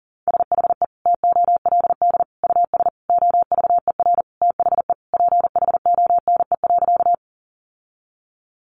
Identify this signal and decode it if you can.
Morse code, heard as a keyed tone, about 40 wpm, 700 Hz. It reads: SHE TOLD US OVER THE PHONE.
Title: Sentences from Top 1100 Words 40wpm